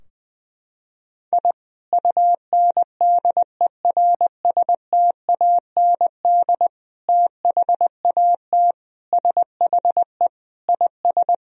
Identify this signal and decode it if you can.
Morse code, heard as a keyed tone, about 21 wpm, 700 Hz. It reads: I UNDERSTAND THAT SHE IS
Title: Sentences from Top 1000 Words - Review Part 3 - 20wpm